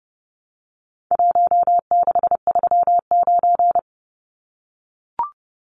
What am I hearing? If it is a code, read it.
Morse code, heard as a keyed tone, about 30 wpm, 700 Hz. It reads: 1639
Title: Numbers - Contest Style - 4-Digit 30wpm